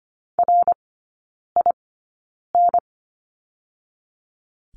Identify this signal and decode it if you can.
Morse code, heard as a keyed tone, about 25 wpm, spaced as for 9 wpm, 690 Hz. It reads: RIN
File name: Lesson 15e - Number 5 - Three Characters 25wpm